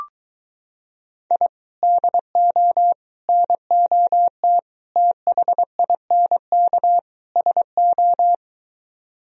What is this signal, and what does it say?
Morse code, heard as a keyed tone, about 23 wpm, 700 Hz. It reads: I DO NOT THINK SO